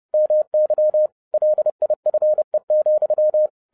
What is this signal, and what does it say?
Morse code, heard as a keyed tone, about 30 wpm, 610 Hz. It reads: MY LIFE,
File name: 2012-01-mp3 / n12-I DIDNT MISS THE SOLAR CYCLE 24-30wpm